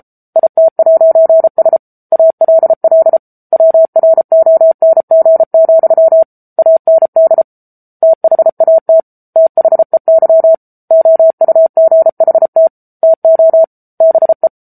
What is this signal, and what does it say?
Morse code, heard as a keyed tone, about 34 wpm, 650 Hz. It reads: IT'S ALL WRONG, AND THAT THEY OUGHT TO BE
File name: Book Two / 07-Chapter 7 - 33 WPM